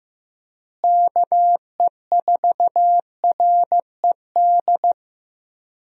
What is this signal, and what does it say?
Morse code, heard as a keyed tone, about 15 wpm, 700 Hz. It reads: KE4RED
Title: USA Call Signs - 6 Characters - Repeated 3x - 15wpm